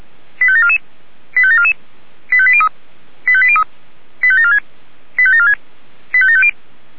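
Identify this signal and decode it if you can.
Five-tone selective call (ZVEI), heard as quick runs of five tones, tones of about 70 ms. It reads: 85744 85744 85803 85803 85746 85746 85750
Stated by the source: ZVEI five-tone selective call 85942 received via SDR.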